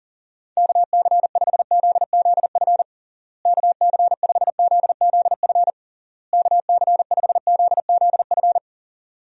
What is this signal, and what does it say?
Morse code, heard as a keyed tone, about 37 wpm, 700 Hz. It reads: KC5ZZF KC5ZZF KC5ZZF
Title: USA Call Signs - 6 Characters - Repeated 3x - 40wpm